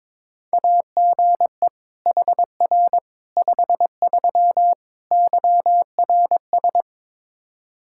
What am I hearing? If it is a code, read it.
Morse code, heard as a keyed tone, about 22 wpm, 700 Hz. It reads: AGE HR 53 YRS
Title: Practice QSO - Exchange 3 - More Info - 22wpm